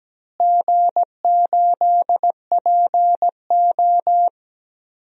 Morse code, heard as a keyed tone, about 17 wpm, 700 Hz. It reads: G8PO
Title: DXCC-100 Call Signs - 4 Characters - Repeated 3x - 17wpm